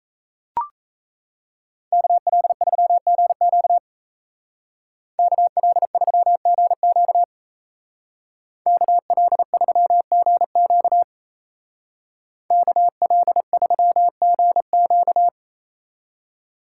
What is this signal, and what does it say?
Morse code, heard as a keyed tone, about 34 wpm, 700 Hz. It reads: KL3GQ KL3GQ KL3GQ KL3GQ
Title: Speed Racing - USA 2x2 Call Signs 28wpm